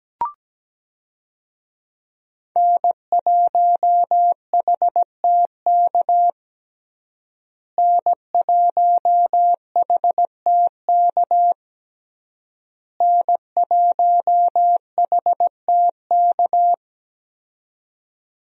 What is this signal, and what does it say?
Morse code, heard as a keyed tone, about 17 wpm, 700 Hz. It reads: N1HTK N1HTK N1HTK
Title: USA Call Signs - 5 Characters - Repeated 3x - 17wpm